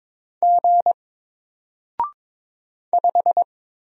Morse code, heard as a keyed tone, about 22 wpm, 700 Hz. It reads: G 5
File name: Single-Letter-Number-22wpm